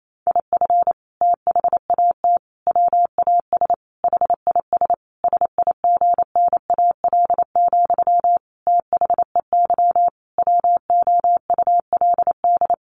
Morse code, heard as a keyed tone, about 28 wpm, 700 Hz. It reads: IF THAT WAS HIS SIGNAL, THEY WOULD
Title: Sentences from Top 1400 Words 28wpm